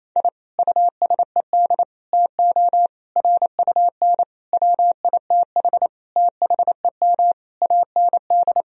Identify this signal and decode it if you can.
Morse code, heard as a keyed tone, about 28 wpm, 700 Hz. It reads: I USED TO RUN WITH THEM AND
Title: Sentences from Top 1000 Words - Review Part 4 - 28wpm